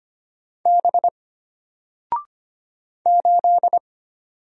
Morse code, heard as a keyed tone, about 25 wpm, 700 Hz. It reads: B 8
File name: Single-Letter-Number-25wpm